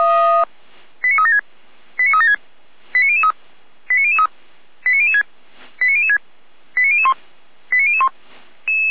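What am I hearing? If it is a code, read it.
Five-tone selective call (ZVEI), heard as quick runs of five tones, tones of about 70 ms. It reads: 89376 89376 89003 89003 89006 89006 89001 89001
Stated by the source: ZVEI five-tone selective call received via SDR.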